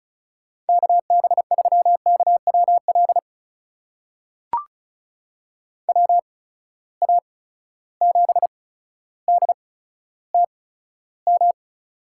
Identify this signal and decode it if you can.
Morse code, heard as a keyed tone, about 34 wpm, spaced as for 15 wpm, 700 Hz. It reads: KB3KWL WA7DTM
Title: USA Call Signs - 6 Characters - 8x Character Spacing - 35wpm